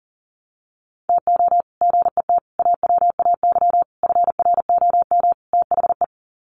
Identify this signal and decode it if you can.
Morse code, heard as a keyed tone, about 40 wpm, 700 Hz. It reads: TO GET AWAY FROM THE
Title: Sets of 5 Words Encore2 40wpm